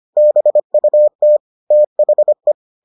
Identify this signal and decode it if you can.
Morse code, heard as a keyed tone, about 25 wpm, 600 Hz. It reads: BUT THE